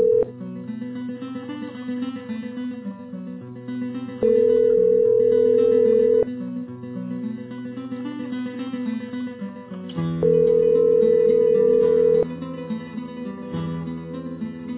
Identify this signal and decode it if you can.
Telephone call progress tone (North America), heard as a telephone network's tone, about 440, 480 Hz, with noise. Ringback tone.